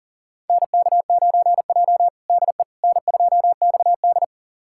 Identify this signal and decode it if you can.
Morse code, heard as a keyed tone, about 37 wpm, 700 Hz. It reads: NK9J DE N2XD
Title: USA Callsign DE Callsign - 4 Characters - 40wpm